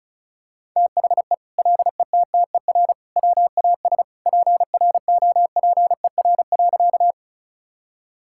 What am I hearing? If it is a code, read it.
Morse code, heard as a keyed tone, about 36 wpm, 700 Hz. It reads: THE LETTER WAS PROPER.
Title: Sentences from Top 1000 Words - Review Part 5 - 35wpm